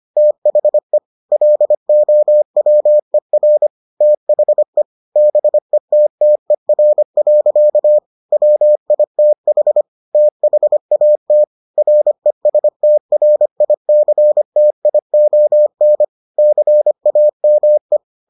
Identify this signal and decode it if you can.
Morse code, heard as a keyed tone, about 25 wpm, 600 Hz. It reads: THE LOWER THE BETTER. WITH THAT RESTRICTION CAME